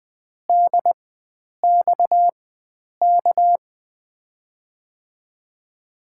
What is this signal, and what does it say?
Morse code, heard as a keyed tone, about 20 wpm, spaced as for 9 wpm, 700 Hz. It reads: DXK
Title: USA Call Signs - 5 Characters - 4x Character Spacing - 20wpm